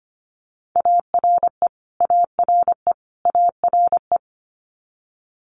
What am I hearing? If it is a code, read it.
Morse code, heard as a keyed tone, about 25 wpm, 700 Hz. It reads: ARE ARE ARE
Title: Multi-letter Phonemes Repeated 3x 25wpm